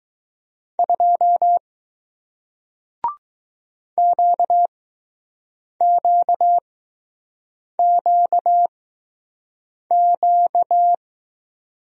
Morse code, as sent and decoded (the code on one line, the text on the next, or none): ..--- / --.- --.- --.- --.-
2 QQQQ